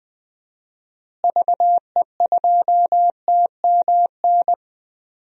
Morse code, heard as a keyed tone, about 20 wpm, 700 Hz. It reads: VE2TMN
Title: DXCC-100 Call Signs - 6 Characters - Repeated 2x - 20wpm